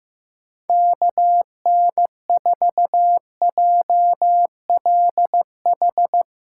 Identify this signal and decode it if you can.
Morse code, heard as a keyed tone, about 16 wpm, 700 Hz. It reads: KN4JLH
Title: Speed Racing - USA 2x3 Call Signs 15wpm